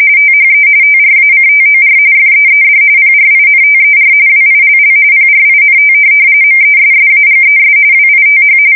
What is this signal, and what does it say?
RTTY, about 45.45 baud. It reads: THE TRACTOR FEED PRINTER WILL PRINT MESSAGES WITH
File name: fsk170 / teletype-corp